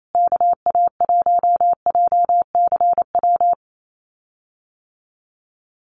Morse code, heard as a keyed tone, about 28 wpm, 700 Hz. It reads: KA1JCW
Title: USA Call Signs Repeated 2x 28wpm